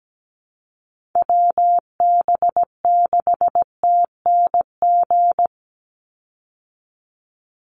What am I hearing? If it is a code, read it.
Morse code, heard as a keyed tone, about 17 wpm, 700 Hz. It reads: WB6TNG
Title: USA Call Signs 17wpm